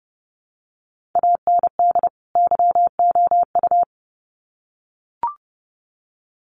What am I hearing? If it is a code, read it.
Morse code, heard as a keyed tone, about 30 wpm, 700 Hz. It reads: AND YOU
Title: Sets of 2 Words 30wpm